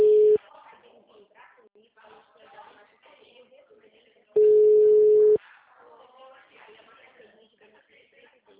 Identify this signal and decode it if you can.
Ringback tone (Europe, 425 Hz), with speech.